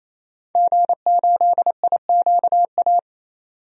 Morse code, heard as a keyed tone, about 28 wpm, 700 Hz. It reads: G8IQA
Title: DXCC-100 Call Signs - 5 Characters - 28wpm